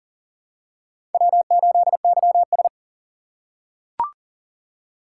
Morse code, heard as a keyed tone, about 40 wpm, 700 Hz. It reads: W8YS
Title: USA Call Signs - 4 Characters - 4x Character Spacing - 40wpm